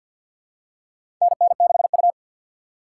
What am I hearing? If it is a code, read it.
Morse code, heard as a keyed tone, about 52 wpm, 700 Hz. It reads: NN6U
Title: USA Call Signs - 4 Characters - 6x Character Spacing - 50wpm